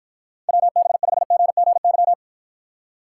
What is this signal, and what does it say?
Morse code, heard as a keyed tone, about 51 wpm, 700 Hz. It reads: WB5GGY